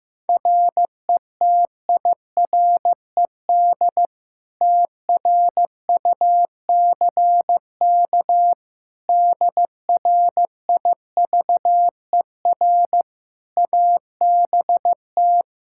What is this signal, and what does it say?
Morse code, heard as a keyed tone, about 15 wpm, 700 Hz. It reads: RETIRED TRUCK DRIVER ABT
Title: Practice QSO - Exchange 3 - More Info - 15wpm